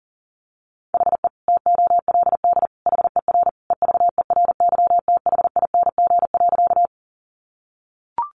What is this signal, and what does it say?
Morse code, heard as a keyed tone, about 40 wpm, 700 Hz. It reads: HE TOLD HER EVERYTHING.
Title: Sentences from Top 700 Words 40wpm